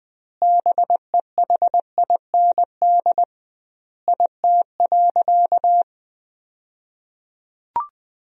Morse code, heard as a keyed tone, about 21 wpm, 700 Hz. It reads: BEHIND IT.
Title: 2x Word-Spacing - Sentences from Top 500 Words - 20wpm